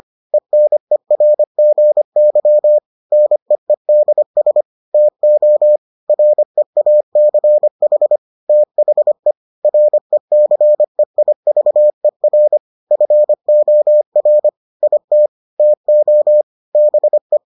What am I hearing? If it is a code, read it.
Morse code, heard as a keyed tone, about 25 wpm, 600 Hz. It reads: ENERGY NEEDS TO REACH THE RECEIVER FOR IT TO BE